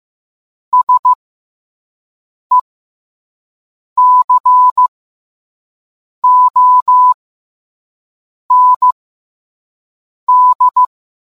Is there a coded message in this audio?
Yes: Morse code from 0.7 to 10.8 s, heard as a keyed tone, about 15 wpm, spaced as for 5 wpm, 1000 Hz. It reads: SECOND